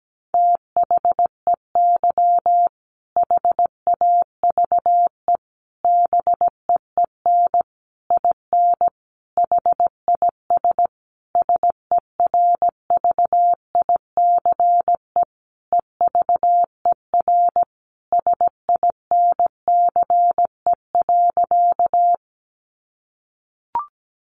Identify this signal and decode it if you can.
Morse code, heard as a keyed tone, about 17 wpm, 700 Hz. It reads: THEY HAVE BEEN IN HIS SERVICE EVER SINCE.
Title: Sentences from Top 700 Words 17wpm